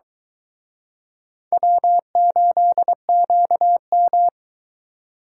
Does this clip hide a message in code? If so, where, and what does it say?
Yes: Morse code from 1.5 to 4.3 s, heard as a keyed tone, about 22 wpm, 700 Hz. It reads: W8QM